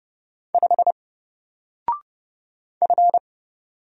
Morse code, heard as a keyed tone, about 30 wpm, 700 Hz. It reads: 5 F
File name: Single-Letter-Number-30wpm